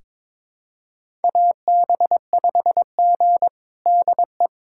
Morse code, heard as a keyed tone, about 21 wpm, 700 Hz. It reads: AB5G DE
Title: USA Callsign DE Callsign - 4 Characters - 22wpm